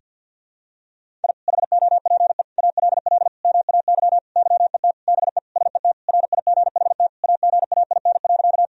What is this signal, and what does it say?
Morse code, heard as a keyed tone, about 51 wpm, 700 Hz. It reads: I HOPE ALL MAY YET BE SET RIGHT AGAIN.
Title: Sentences from Top 1000 Words - Review Part 1 - 50wpm